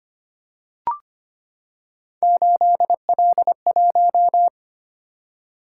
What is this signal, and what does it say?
Morse code, heard as a keyed tone, about 25 wpm, 700 Hz. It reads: 8L1
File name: Three Letter-Number Combinations - 25wpm